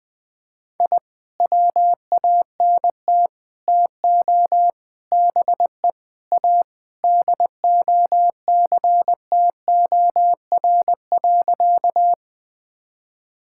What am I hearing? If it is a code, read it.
Morse code, heard as a keyed tone, about 20 wpm, 700 Hz. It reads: I WANT TO BE A DOCTOR.
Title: Sentences with 3-Word Phrases - 20wpm